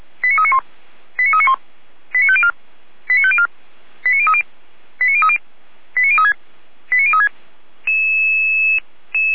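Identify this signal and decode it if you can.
Five-tone selective call (ZVEI), heard as quick runs of five tones, tones of about 70 ms. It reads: 89391 89391 89504 89504 89030 89030 89036 89036
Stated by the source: Received on an SDR; ZVEI five-tone selective call.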